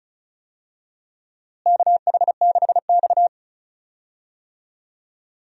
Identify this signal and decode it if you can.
Morse code, heard as a keyed tone, about 34 wpm, 700 Hz. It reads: KH6X
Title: DXCC-100 Call Signs - 4 Characters - Repeated 2x - 35wpm